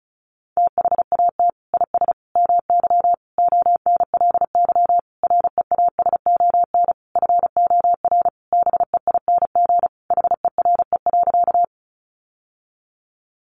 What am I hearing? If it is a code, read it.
Morse code, heard as a keyed tone, about 34 wpm, 700 Hz. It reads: THAT IS MY ONLY REASON FOR BEING HERE.
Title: Sentences from Top 700 Words 35wpm